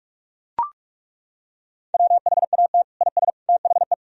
Morse code, heard as a keyed tone, about 41 wpm, 700 Hz. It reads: WHAT IS THE
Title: Sentences from Top 1000 Words - Review Part 3 - 45wpm